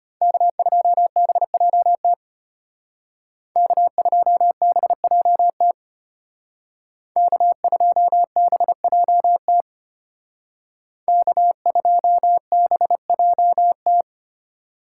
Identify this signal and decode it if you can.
Morse code, heard as a keyed tone, about 32 wpm, 700 Hz. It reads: K2BJT K2BJT K2BJT K2BJT